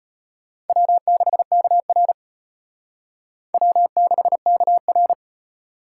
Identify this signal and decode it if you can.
Morse code, heard as a keyed tone, about 32 wpm, 700 Hz. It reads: W6KR W6KR